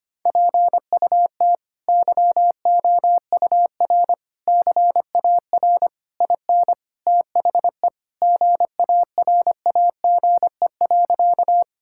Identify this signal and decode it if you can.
Morse code, heard as a keyed tone, about 25 wpm, 700 Hz. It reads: PUT YOUR CAR IN THE GARAGE.
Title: Sentences from Top 1600 Words 25wpm